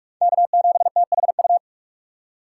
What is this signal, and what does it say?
Morse code, heard as a keyed tone, about 46 wpm, 690 Hz. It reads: K7THU
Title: USA Call Signs - 4x Character Spacing - 45wpm